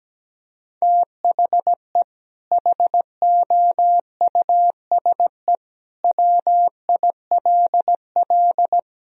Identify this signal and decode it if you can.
Morse code, heard as a keyed tone, about 17 wpm, 700 Hz. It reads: THE HOUSE WILL
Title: Sentences from Top 1000 Words - Review Part 2 - 17wpm